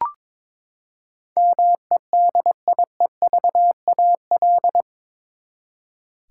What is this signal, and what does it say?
Morse code, heard as a keyed tone, about 22 wpm, 700 Hz. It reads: MEDIEVAL
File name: practice / 22wpm / Eight Letter Words - 22wpm